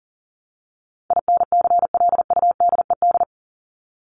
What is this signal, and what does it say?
Morse code, heard as a keyed tone, about 40 wpm, 700 Hz. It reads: INCLUDED
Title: ED Suffix - Top 100 Words with Base Words 40wpm